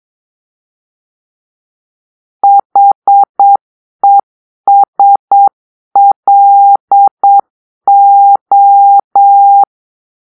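Morse code, heard as a keyed tone, about 8 wpm, 800 Hz. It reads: HESLO
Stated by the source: Morse-code cipher made by a player team for the Kachna puzzle game.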